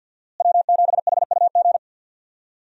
Morse code, heard as a keyed tone, about 51 wpm, 700 Hz. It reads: W7HUG
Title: USA Call Signs - 5 Characters - Repeated 2x - 50wpm